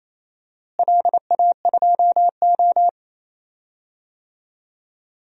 Morse code, heard as a keyed tone, about 28 wpm, 700 Hz. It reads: LA2O
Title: DXCC-100 Call Signs - 4 Characters - Repeated 3x - 28wpm